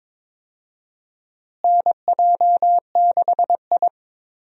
Morse code, heard as a keyed tone, about 22 wpm, 700 Hz. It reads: NJ6I
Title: USA Call Signs - 4 Characters - Repeated 2x - 22wpm